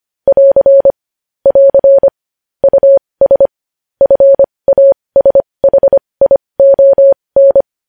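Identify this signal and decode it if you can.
Morse code, heard as a keyed tone, about 25 wpm, spaced as for 20 wpm, 550 Hz. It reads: <AR> <AR> US FASHION
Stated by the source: Morse news headlines sent at 20, opening with vvv.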